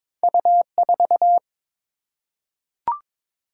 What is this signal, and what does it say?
Morse code, heard as a keyed tone, about 22 wpm, 700 Hz. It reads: U4
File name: practice / 22wpm / Two Letter-Number Combinations - 22wpm